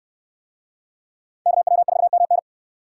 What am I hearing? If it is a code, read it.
Morse code, heard as a keyed tone, about 65 wpm, 700 Hz. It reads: KW4MN